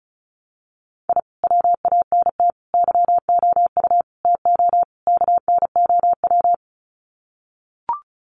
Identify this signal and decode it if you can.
Morse code, heard as a keyed tone, about 34 wpm, 700 Hz. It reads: I WANT YOU TO KNOW
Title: Sets of 5 Words Encore 35wpm